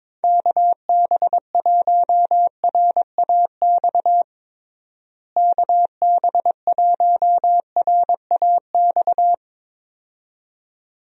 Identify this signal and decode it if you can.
Morse code, heard as a keyed tone, about 22 wpm, 700 Hz. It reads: KB1RAX KB1RAX
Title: USA Call Signs - 6 Characters - Repeated 2x - 22wpm